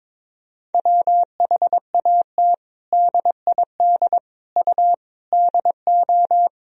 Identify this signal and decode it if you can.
Morse code, heard as a keyed tone, about 22 wpm, 700 Hz. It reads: WHAT DID U DO
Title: Sentences from Top 100 Words and Abbreviations 22wpm